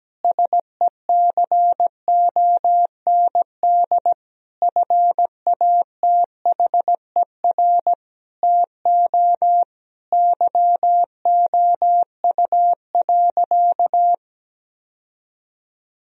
Morse code, heard as a keyed tone, about 17 wpm, 700 Hz. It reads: SECOND FATHER TO YOU.
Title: Sentences from Top 1000 Words - Review Part 1 - 17wpm